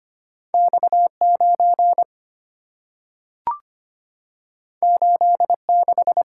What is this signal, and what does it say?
Morse code, heard as a keyed tone, about 25 wpm, 690 Hz. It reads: X9 86
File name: Two Letter-Number Combinations - 25wpm